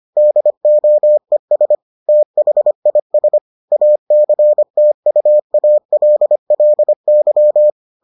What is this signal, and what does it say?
Morse code, heard as a keyed tone, about 25 wpm, 600 Hz. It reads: DOES THIS ACTUALLY